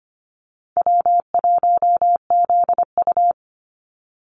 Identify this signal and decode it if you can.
Morse code, heard as a keyed tone, about 25 wpm, 700 Hz. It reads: W1ZU
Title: NA SOTA Activator Callsigns 25wpm